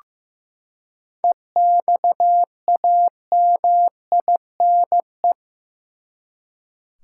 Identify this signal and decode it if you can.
Morse code, heard as a keyed tone, about 15 wpm, 700 Hz. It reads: EXAMINE
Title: Seven Letter Words - 15wpm